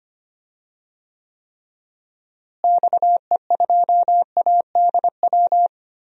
Morse code, heard as a keyed tone, about 25 wpm, 700 Hz. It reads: XE2ADW
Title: DXCC-100 Call Signs - 6 Characters - Repeated 3x - 25wpm